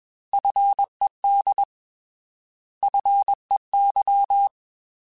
Morse code, heard as a keyed tone, about 21 wpm, 800 Hz. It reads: FED FEY
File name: Alphabetical / Alphabetical mp3s / Fs @20